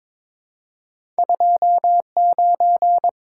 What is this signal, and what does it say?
Morse code, heard as a keyed tone, about 22 wpm, 690 Hz. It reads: 29
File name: practice / 22wpm / Two Letter-Number Combinations - 22wpm